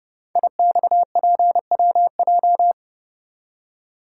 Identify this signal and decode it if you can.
Morse code, heard as a keyed tone, about 30 wpm, 700 Hz. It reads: IXPWJ
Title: Code Groups - 5 Letters - 30wpm